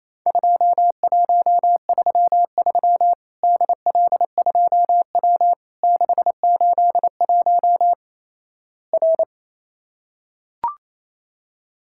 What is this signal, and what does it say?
Morse code, heard as a keyed tone, about 28 wpm, 700 Hz. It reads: 2133 DL2W 681 R
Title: Worked All Europe DX Contest - QTC Exchange - 28wpm